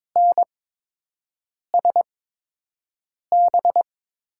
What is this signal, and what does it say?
Morse code, heard as a keyed tone, about 22 wpm, spaced as for 6 wpm, 700 Hz. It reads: NSB
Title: USA Call Signs - 5 Characters - 8x Character Spacing - 22wpm